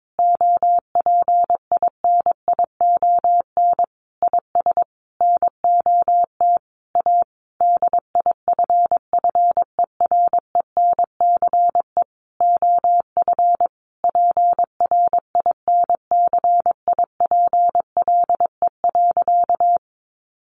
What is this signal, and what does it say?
Morse code, heard as a keyed tone, about 22 wpm, 700 Hz. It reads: OPINION IS NOT A DIFFERENCE OF PRINCIPLE.
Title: Sentences from Top 1400 Words 22wpm